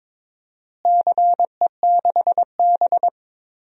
Morse code, heard as a keyed tone, about 23 wpm, 700 Hz. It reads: CE6B
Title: DXCC-100 Call Signs - 4 Characters - Repeated 2x - 22wpm